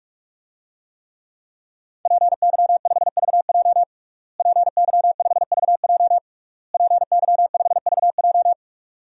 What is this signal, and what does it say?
Morse code, heard as a keyed tone, about 45 wpm, 700 Hz. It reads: PY5VJ PY5VJ PY5VJ
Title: DXCC-100 Call Signs - 5 Characters - Repeated 3x - 45wpm